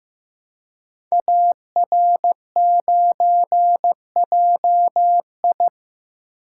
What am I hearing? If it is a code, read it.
Morse code, heard as a keyed tone, about 15 wpm, 700 Hz. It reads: AR9JI
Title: Code Groups - 5 Letter:Numbers - 15wpm